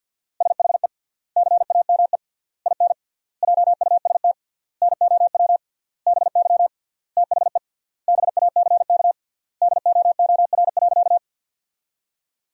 Morse code, heard as a keyed tone, about 51 wpm, 700 Hz. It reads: SHE CAME IN JUST NOW BY THE BACK DOOR.
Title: 3x Word-Spacing - Sentences from Top 300 Words - 50wpm